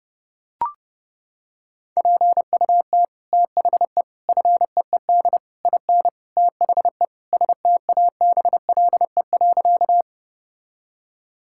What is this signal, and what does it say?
Morse code, heard as a keyed tone, about 30 wpm, 700 Hz. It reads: PUT THE FEED IN THE STABLE.
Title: Sentences from Top 1600 Words 30wpm